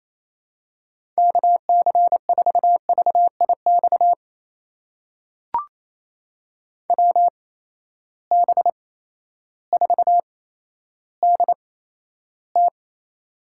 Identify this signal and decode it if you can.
Morse code, heard as a keyed tone, about 28 wpm, spaced as for 13 wpm, 700 Hz. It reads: KC4VIX WB4DT
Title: USA Call Signs - 8x Character Spacing - 28wpm